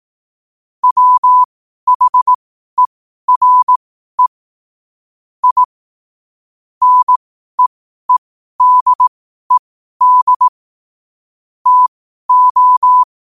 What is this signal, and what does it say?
Morse code, heard as a keyed tone, about 18 wpm, spaced as for 12 wpm, 1000 Hz. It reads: WHERE I NEEDED TO
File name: qotdfiles / 12wpm